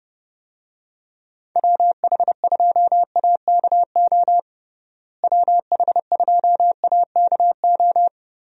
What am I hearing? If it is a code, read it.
Morse code, heard as a keyed tone, about 30 wpm, 700 Hz. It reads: WH2AKO WH2AKO
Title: USA Call Signs - 6 Characters - Repeated 2x - 30wpm